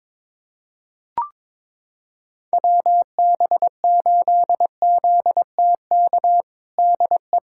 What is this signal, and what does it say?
Morse code, heard as a keyed tone, about 22 wpm, 700 Hz. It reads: WB8ZTK DE
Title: USA Callsign DE Callsign - 6 Characters - 22wpm